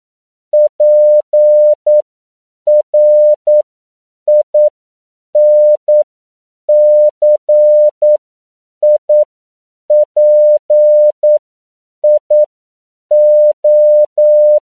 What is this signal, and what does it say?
Morse code, heard as a keyed tone, about 9 wpm, spaced as for 7 wpm, 600 Hz. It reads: PRINCIPIO
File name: CW / 10PPM